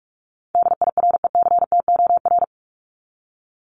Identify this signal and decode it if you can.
Morse code, heard as a keyed tone, about 44 wpm, 700 Hz. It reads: DIRECTOR
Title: Top 1251-1500 Words 45wpm